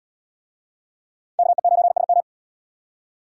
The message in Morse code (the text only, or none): KE9SM